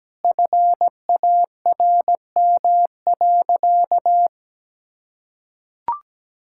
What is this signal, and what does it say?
Morse code, heard as a keyed tone, about 17 wpm, 700 Hz. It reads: FARM.
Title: Sentences from Top 1000 Words - Review Part 6 - 17wpm